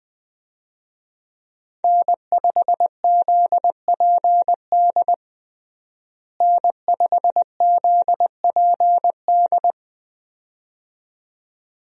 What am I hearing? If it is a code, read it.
Morse code, heard as a keyed tone, about 20 wpm, 700 Hz. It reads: N5ZPD N5ZPD
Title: USA Call Signs - 5 Characters - Repeated 2x - 20wpm